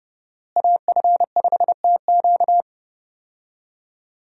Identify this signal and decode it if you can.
Morse code, heard as a keyed tone, about 31 wpm, 700 Hz. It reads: AF5TQ
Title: USA Call Signs - 5 Characters - Repeated 2x - 30wpm